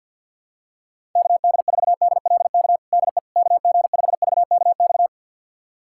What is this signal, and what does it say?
Morse code, heard as a keyed tone, about 51 wpm, 700 Hz. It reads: KD4DLK DE KG5VKX
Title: USA Callsign DE Callsign - 6 Characters - 50wpm